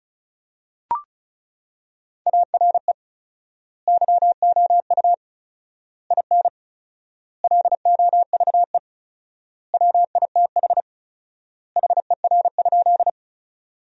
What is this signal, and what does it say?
Morse code, heard as a keyed tone, about 36 wpm, 700 Hz. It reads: ARE YOU IN LOVE WITH HER?
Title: 4x Word-Spacing - Sentences from Top 400 Words - 35wpm